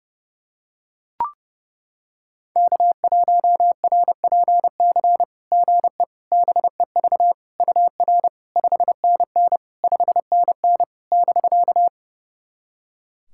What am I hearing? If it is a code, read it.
Morse code, heard as a keyed tone, about 30 wpm, 700 Hz. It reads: K1RPC GE BEV UR 5NN 5NN <BK>
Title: SOTA QSO Practice 30wpm